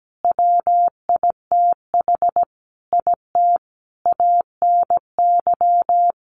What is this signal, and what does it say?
Morse code, heard as a keyed tone, about 17 wpm, 700 Hz. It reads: WITH IT ANY
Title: Sentences from Top 1500 Words 17wpm